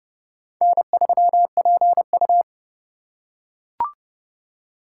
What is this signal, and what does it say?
Morse code, heard as a keyed tone, about 28 wpm, 700 Hz. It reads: N3PU